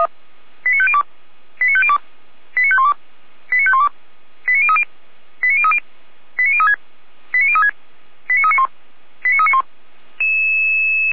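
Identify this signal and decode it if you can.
Five-tone selective call (ZVEI), heard as quick runs of five tones, tones of about 70 ms. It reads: 89502 89502 89512 89512 89030 89030 89036 89036 89391 89391
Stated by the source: ZVEI five-tone selective call 88177 received on an SDR.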